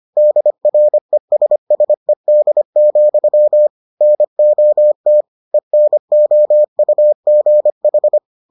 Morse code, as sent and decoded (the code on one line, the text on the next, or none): -.. .-. ... ... -.. --..-- / -. --- - / . -. --- ..- --. ....
DRESSED, NOT ENOUGH